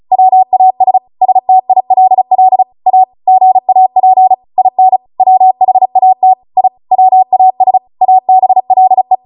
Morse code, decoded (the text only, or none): WAS STILL A GAP IN WHAT I WAS ABLE